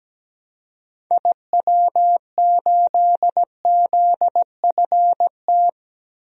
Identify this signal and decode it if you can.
Morse code, heard as a keyed tone, about 17 wpm, 700 Hz. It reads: IW8ZFT
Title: DXCC-100 Call Signs - 6 Characters - 17wpm